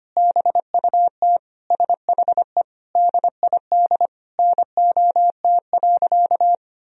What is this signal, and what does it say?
Morse code, heard as a keyed tone, about 25 wpm, 700 Hz. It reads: BUT SHE DID NOT.
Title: Sentences from Top 1000 Words - Review Part 1 - 25wpm